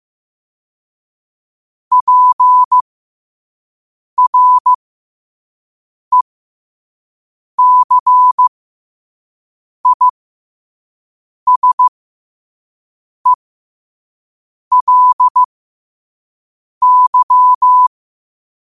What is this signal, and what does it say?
Morse code, heard as a keyed tone, about 15 wpm, spaced as for 5 wpm, 1000 Hz. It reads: PRECISELY
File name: qotdfiles / 5wpm